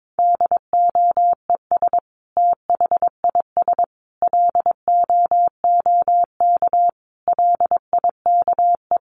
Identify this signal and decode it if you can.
Morse code, heard as a keyed tone, about 22 wpm, 700 Hz. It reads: DOES THIS LOOK LIKE
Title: Sentences from Top 1500 Words 22wpm